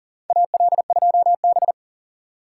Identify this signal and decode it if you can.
Morse code, heard as a keyed tone, about 40 wpm, 700 Hz. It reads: AL2B
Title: USA Call Signs - 4 Characters - 2x Character Spacing - 40wpm